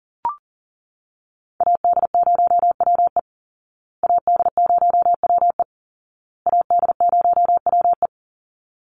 Morse code, heard as a keyed tone, about 40 wpm, 700 Hz. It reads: AD0WE AD0WE AD0WE
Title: Call Signs Repeated 2x - 40wpm